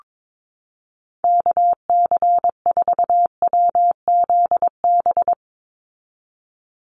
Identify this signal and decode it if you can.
Morse code, heard as a keyed tone, about 22 wpm, 700 Hz. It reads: KC4WZB
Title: NA SOTA Activator Callsigns 22wpm